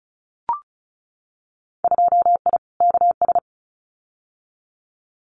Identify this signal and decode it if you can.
Morse code, heard as a keyed tone, about 36 wpm, 700 Hz. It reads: 2I KS
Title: Winter Field Day Exchanges Encore 35wpm